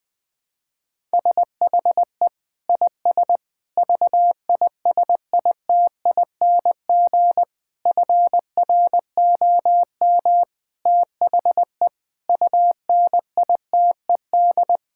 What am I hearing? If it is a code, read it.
Morse code, heard as a keyed tone, about 20 wpm, 700 Hz. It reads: SHE IS VISITING FROM THE UNITED
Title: Sentences with 4-Word Phrases - 20wpm